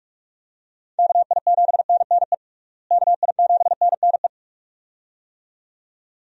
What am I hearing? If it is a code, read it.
Morse code, heard as a keyed tone, about 46 wpm, 700 Hz. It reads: KI7NNE KI7NNE
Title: USA Call Signs - 6 Characters - Repeated 2x - 45wpm